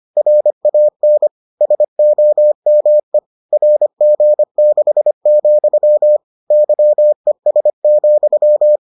Morse code, heard as a keyed tone, about 25 wpm, 600 Hz. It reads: RAN SOME RG6, YES,